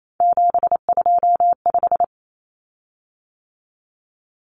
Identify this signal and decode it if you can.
Morse code, heard as a keyed tone, about 28 wpm, 700 Hz. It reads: 725
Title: Common Cut Numbers - 3-Digit 28wpm